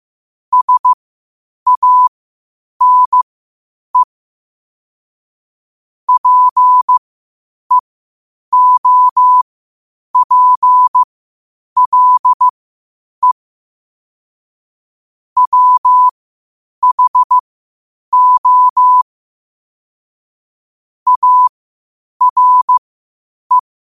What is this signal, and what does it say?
Morse code, heard as a keyed tone, about 15 wpm, spaced as for 8 wpm, 1000 Hz. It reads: SANE PEOPLE WHO ARE
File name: qotdfiles / 8wpm